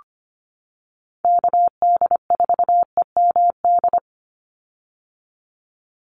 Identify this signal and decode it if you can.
Morse code, heard as a keyed tone, about 25 wpm, 700 Hz. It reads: KD4EMD
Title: USA Call Signs 25wpm